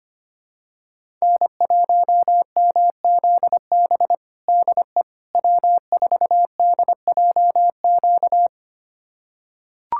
Morse code, heard as a keyed tone, about 25 wpm, 700 Hz. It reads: N1MZB DE W4DJQ
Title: USA Callsign DE Callsign - 5 Characters - 25wpm